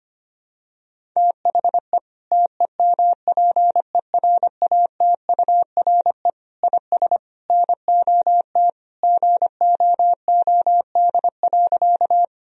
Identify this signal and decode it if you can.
Morse code, heard as a keyed tone, about 25 wpm, 700 Hz. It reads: THE TEMPERATURE IS NOT GOOD.
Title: Sentences from Top 1000 Words - Review Part 4 - 25wpm